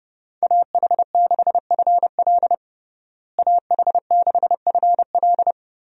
Morse code, heard as a keyed tone, about 30 wpm, 700 Hz. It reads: AH6FL AH6FL
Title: USA Call Signs - 5 Characters - Repeated 3x - 30wpm